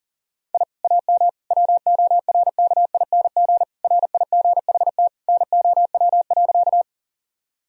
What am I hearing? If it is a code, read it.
Morse code, heard as a keyed tone, about 40 wpm, 700 Hz. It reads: I AM WORKING RIGHT NOW.
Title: Sentences from Top 1000 Words - Review Part 4 - 40wpm